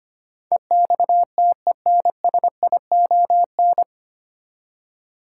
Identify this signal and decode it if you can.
Morse code, heard as a keyed tone, about 25 wpm, 700 Hz. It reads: EXTENSION
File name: Nine Letter Words - 25wpm